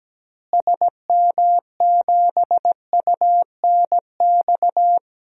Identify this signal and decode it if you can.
Morse code, heard as a keyed tone, about 17 wpm, 700 Hz. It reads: SM7UNX
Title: DXCC-100 Call Signs - 6 Characters - Repeated 3x - 17wpm